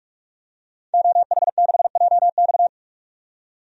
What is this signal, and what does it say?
Morse code, heard as a keyed tone, about 47 wpm, 700 Hz. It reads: OH6JX
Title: DXCC-100 Call Signs - 5 Characters - 45wpm